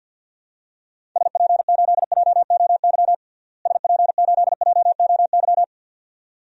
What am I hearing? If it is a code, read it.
Morse code, heard as a keyed tone, about 51 wpm, 700 Hz. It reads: SP8JOY SP8JOY
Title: DXCC-100 Call Signs - 6 Characters - Repeated 2x - 50wpm